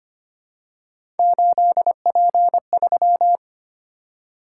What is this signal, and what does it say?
Morse code, heard as a keyed tone, about 25 wpm, 700 Hz. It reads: 8P3
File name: Three Letter-Number Combinations - 25wpm